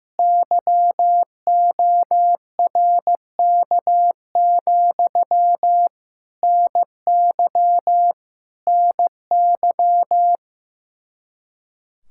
Morse code, heard as a keyed tone, about 15 wpm, 700 Hz. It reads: YORK, NY NY
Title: Speed-Racing Common QSO Phrases Encore - 15wpm